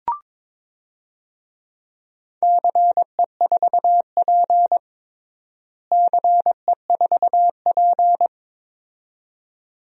Morse code, heard as a keyed tone, about 22 wpm, 700 Hz. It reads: CE4P CE4P
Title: DXCC-100 Call Signs - 4 Characters - Repeated 2x - 22wpm